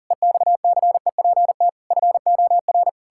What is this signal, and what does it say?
Morse code, heard as a keyed tone, about 37 wpm, 700 Hz. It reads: EXCEPT FOR
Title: Pangram Challenge v2 - 40wpm